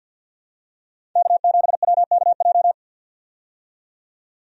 Morse code, heard as a keyed tone, about 52 wpm, 700 Hz. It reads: K7WKJ
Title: USA Call Signs - 5 Characters - 50wpm